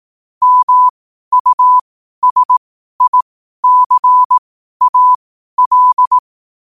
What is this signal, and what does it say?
Morse code, heard as a keyed tone, about 18 wpm, spaced as for 12 wpm, 1000 Hz. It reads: MUSICAL
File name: qotdfiles / 12wpm